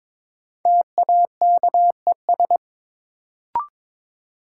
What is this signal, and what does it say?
Morse code, heard as a keyed tone, about 22 wpm, 700 Hz. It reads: TAKES
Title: Sentences from Top 1000 Words - Review Part 6 - New Word Drill - 22wpm